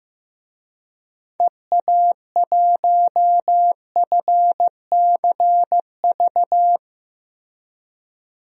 Morse code, heard as a keyed tone, about 15 wpm, 700 Hz. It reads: EA1FCV
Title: DXCC-100 Call Signs - 6 Characters - Repeated 2x - 15wpm